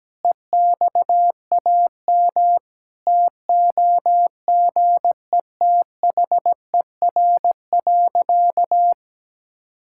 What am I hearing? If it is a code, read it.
Morse code, heard as a keyed tone, about 17 wpm, 700 Hz. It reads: EXAM TOGETHER.
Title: Sentences from Top 1000 Words - Review Part 5 - 17wpm